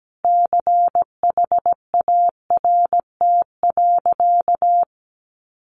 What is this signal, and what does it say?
Morse code, heard as a keyed tone, about 17 wpm, 700 Hz. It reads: CHART.
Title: Sentences from Top 1300 Words 17wpm